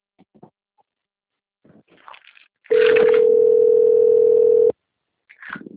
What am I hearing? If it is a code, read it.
Ringback tone (North America), 440 + 480 Hz.